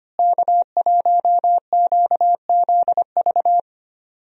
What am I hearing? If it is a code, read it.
Morse code, heard as a keyed tone, about 25 wpm, 700 Hz. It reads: K1QZV
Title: USA Call Signs - 5 Characters - Repeated 2x - 25wpm